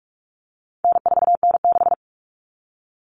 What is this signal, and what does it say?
Morse code, heard as a keyed tone, about 44 wpm, 700 Hz. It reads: N4N6
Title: Common Cut Numbers - 4-Digit 45wpm